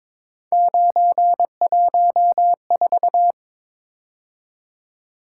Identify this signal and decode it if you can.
Morse code, heard as a keyed tone, about 22 wpm, 700 Hz. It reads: 914
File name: practice / 22wpm / Three Letter-Number Combinations - 22wpm